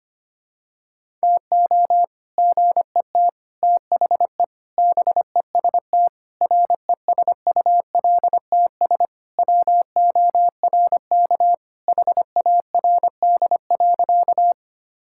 Morse code, heard as a keyed tone, about 25 wpm, 700 Hz. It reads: TO GET THE BEST RESULTS WORK HARD.
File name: practice / 25wpm / Sentences with 3-Word Phrases - 25wpm